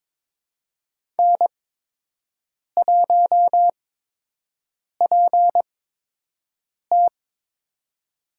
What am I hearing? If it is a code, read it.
Morse code, heard as a keyed tone, about 22 wpm, spaced as for 6 wpm, 700 Hz. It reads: N1PT
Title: USA Call Signs - 4 Characters - 8x Character Spacing - 22wpm